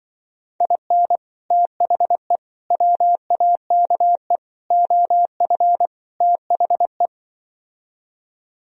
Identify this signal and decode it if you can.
Morse code, heard as a keyed tone, about 24 wpm, 700 Hz. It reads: IN THE WAKE OF THE
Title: Speed-Racing - Sets of 5 Words 20wpm